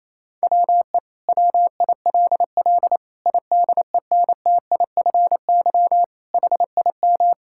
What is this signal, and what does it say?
Morse code, heard as a keyed tone, about 28 wpm, 700 Hz. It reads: WE WILL IDENTIFY HIM
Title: Sentences from Top 1000 Words - Review Part 6 - 28wpm